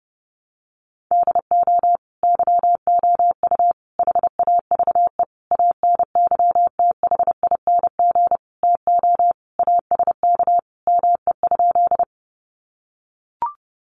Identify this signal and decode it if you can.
Morse code, heard as a keyed tone, about 30 wpm, 700 Hz. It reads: DO YOU HAVE ANYTHING TO ASK ME?
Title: Sentences from Top 1000 Words 30wpm